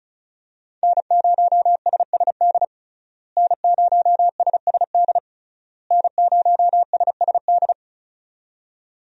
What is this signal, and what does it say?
Morse code, heard as a keyed tone, about 36 wpm, 700 Hz. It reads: N0SSD N0SSD N0SSD
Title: USA Call Signs - 5 Characters - Repeated 3x - 35wpm